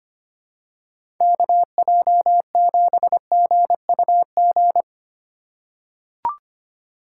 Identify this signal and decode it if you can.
Morse code, heard as a keyed tone, about 25 wpm, 700 Hz. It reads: KJ7GUG
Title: USA Call Signs - 2x Character Spacing - 25wpm